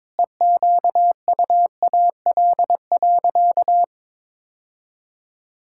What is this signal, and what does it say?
Morse code, heard as a keyed tone, about 22 wpm, 700 Hz. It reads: EQUAL.
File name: Sentences with 9-Word Phrases - 22wpm